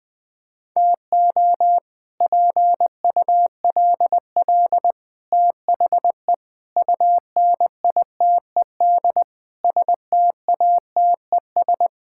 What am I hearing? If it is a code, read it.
Morse code, heard as a keyed tone, about 21 wpm, 700 Hz. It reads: TO PULL THE UNITED STATES
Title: Sets of 8 Words - 20wpm